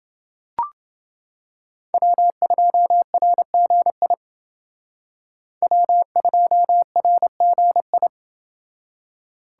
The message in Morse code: W2RGI W2RGI

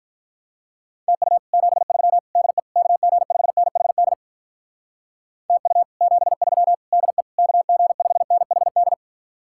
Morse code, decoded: TU 73 DE KG5NHD TU 73 DE KG5NHD

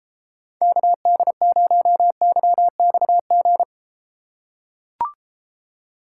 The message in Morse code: KD0YXG

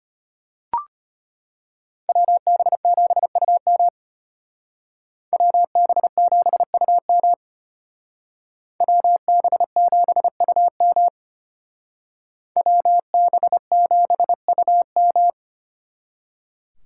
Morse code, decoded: WB7UM WB7UM WB7UM WB7UM

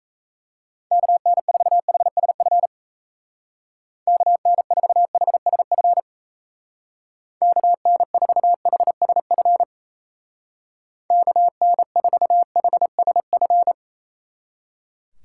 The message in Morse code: KN4HSF KN4HSF KN4HSF KN4HSF

